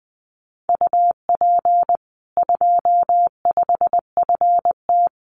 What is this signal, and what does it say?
Morse code, heard as a keyed tone, about 20 wpm, 700 Hz. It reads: UP 25FT
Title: Common QSO Phrases 20wpm